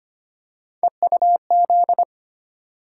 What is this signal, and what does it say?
Morse code, heard as a keyed tone, about 25 wpm, 700 Hz. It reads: EUZ
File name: Three Letter-Number Combinations - 25wpm